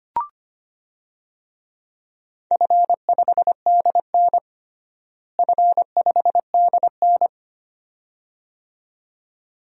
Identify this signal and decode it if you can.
Morse code, heard as a keyed tone, about 25 wpm, 700 Hz. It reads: F5DN F5DN
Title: DXCC-100 Call Signs - 4 Characters - Repeated 2x - 25wpm